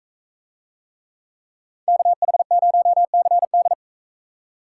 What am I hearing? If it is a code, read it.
Morse code, heard as a keyed tone, about 42 wpm, 700 Hz. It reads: KH0CD